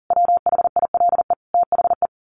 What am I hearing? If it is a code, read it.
Morse code, heard as a keyed tone, about 40 wpm, 700 Hz. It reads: WHILE THE
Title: A Sound Of Thunder 40wpm